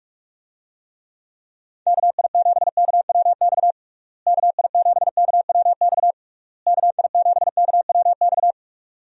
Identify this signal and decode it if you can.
Morse code, heard as a keyed tone, about 46 wpm, 700 Hz. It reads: KI7KWX KI7KWX KI7KWX
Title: USA Call Signs - 6 Characters - Repeated 3x - 45wpm